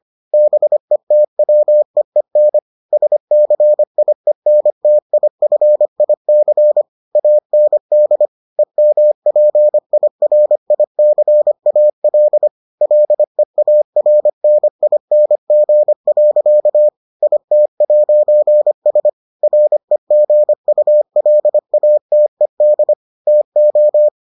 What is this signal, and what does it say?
Morse code, heard as a keyed tone, about 25 wpm, 600 Hz. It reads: BETWEEN SCIENTIFIC AND EMPIRICAL LEARNING. IT'S REGULATED TO